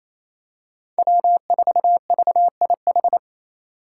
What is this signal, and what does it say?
Morse code, heard as a keyed tone, about 28 wpm, 700 Hz. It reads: W4VIH
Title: Code Groups - 5 Letter:Numbers - 28wpm